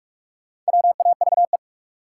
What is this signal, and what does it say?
Morse code, heard as a keyed tone, about 46 wpm, 700 Hz. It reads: WAVE
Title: Sentences from Top 1400 Words - New Word Drill Alt - 45wpm